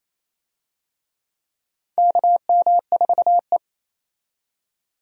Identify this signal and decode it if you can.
Morse code, heard as a keyed tone, about 28 wpm, 700 Hz. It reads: KM4E
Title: USA Call Signs - 4 Characters - 28wpm